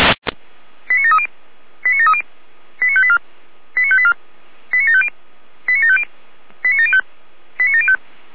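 Five-tone selective call (ZVEI), heard as quick runs of five tones, tones of about 70 ms. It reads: 89730 89730 89584 89584 89750 89750 89705 89705
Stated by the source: ZVEI five-tone call 89720 received by SDR.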